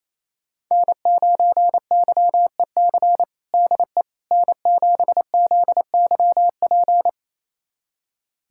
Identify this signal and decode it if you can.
Morse code, heard as a keyed tone, about 28 wpm, 700 Hz. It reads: N9YEC DE N7ZYP
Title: USA Callsign DE Callsign - 5 Characters - 28wpm